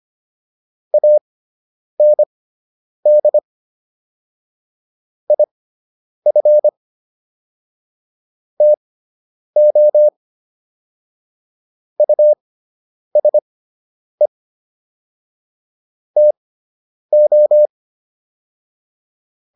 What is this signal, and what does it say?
Morse code, heard as a keyed tone, about 25 wpm, spaced as for 9 wpm, 600 Hz. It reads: AND IF TO USE TO